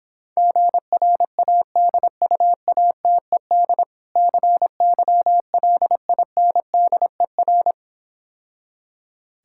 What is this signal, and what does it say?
Morse code, heard as a keyed tone, about 26 wpm, 700 Hz. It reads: GRADUATED CYLINDER